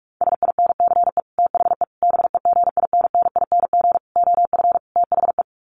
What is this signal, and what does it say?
Morse code, heard as a keyed tone, about 44 wpm, 700 Hz. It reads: SINCE THE BEGINNING OF THE
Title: Sets of 5 Words Encore2 45wpm